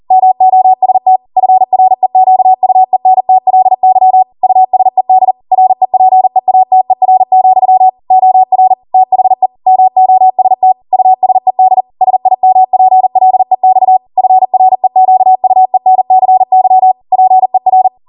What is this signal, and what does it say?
Morse code, heard as a keyed tone, about 40 wpm, 750 Hz. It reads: MOST FREQUENTLY USED REPEATER, OR THE MOST USED SIMPLEX FREQUENCY PER
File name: Morse / 40WPM